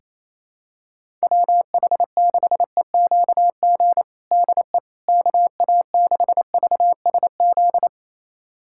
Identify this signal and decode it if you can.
Morse code, heard as a keyed tone, about 28 wpm, 700 Hz. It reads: WH6EQG DE KA6VSZ
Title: USA Callsign DE Callsign - 6 Characters - 28wpm